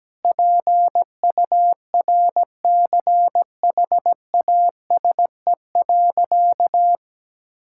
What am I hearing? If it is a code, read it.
Morse code, heard as a keyed tone, about 17 wpm, 700 Hz. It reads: PURCHASE.
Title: Sentences from Top 1000 Words - Review Part 5 - 17wpm